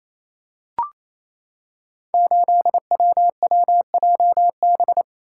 Morse code, heard as a keyed tone, about 28 wpm, 700 Hz. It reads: 8WWJB
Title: Code Groups - 5 Letter:Numbers - 28wpm